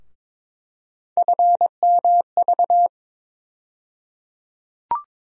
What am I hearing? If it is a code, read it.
Morse code, heard as a keyed tone, about 22 wpm, 700 Hz. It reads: FMV
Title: USA Call Sign Postfixes - 3 Letters 22wpm